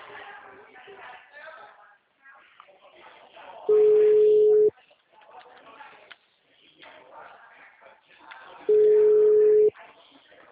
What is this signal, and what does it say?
Ringback tone (Europe, 425 Hz), with speech.
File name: out- -2003- - - .65285